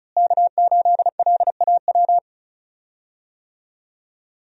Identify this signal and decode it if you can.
Morse code, heard as a keyed tone, about 36 wpm, 700 Hz. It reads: K8LAW
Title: USA Call Signs - 5 Characters - Repeated 3x - 35wpm